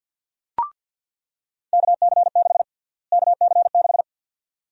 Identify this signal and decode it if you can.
Morse code, heard as a keyed tone, about 49 wpm, 700 Hz. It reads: KC6 KC6
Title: USA Call Sign Prefixes - Repeated 2x - 50wpm